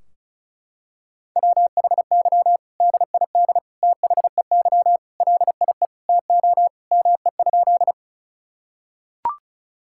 Morse code, heard as a keyed tone, about 36 wpm, 700 Hz. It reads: WHY DID THEY LIE TO ME?
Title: Sentences from Top 1600 Words 35wpm